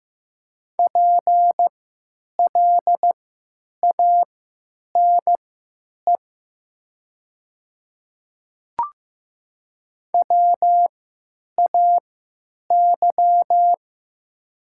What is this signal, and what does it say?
Morse code, heard as a keyed tone, about 15 wpm, spaced as for 8 wpm, 700 Hz. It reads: PLANE WAY
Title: 3x Character Spacing - Top 500 Words - 15wpm